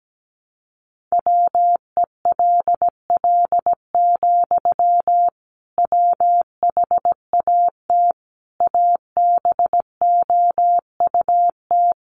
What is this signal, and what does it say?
Morse code, heard as a keyed tone, about 17 wpm, 700 Hz. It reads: WELL, WHAT ABOUT
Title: Sentences from Top 500 Words 17wpm